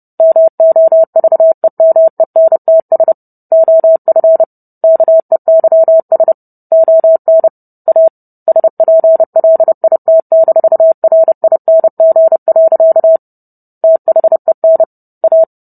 Morse code, heard as a keyed tone, about 30 wpm, 650 Hz. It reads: MOVEMENTS OF KEYS ON A SPLIT-RING. THEN A